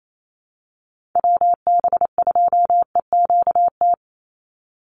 Morse code, heard as a keyed tone, about 28 wpm, 700 Hz. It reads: WB2EQT
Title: USA Call Signs 28wpm